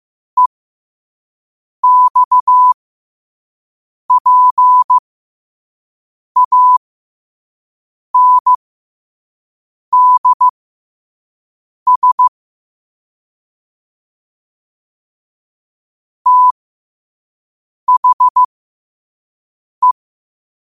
Morse code, heard as a keyed tone, about 15 wpm, spaced as for 5 wpm, 1000 Hz. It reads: EXPANDS THE
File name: qotdfiles / 5wpm